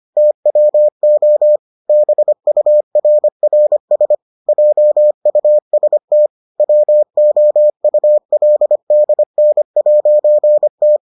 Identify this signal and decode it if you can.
Morse code, heard as a keyed tone, about 25 wpm, 600 Hz. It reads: TWO BURRS JUST WOULDN'T